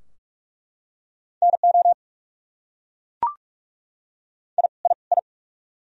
Morse code, heard as a keyed tone, about 46 wpm, 700 Hz. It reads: NO I I I
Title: Top 100 Words Repeated 45wpm